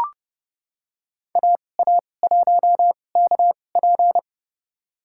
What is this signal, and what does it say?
Morse code, heard as a keyed tone, about 30 wpm, spaced as for 21 wpm, 700 Hz. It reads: AA1KP
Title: USA Call Signs - 5 Characters - 2x Character Spacing - 30wpm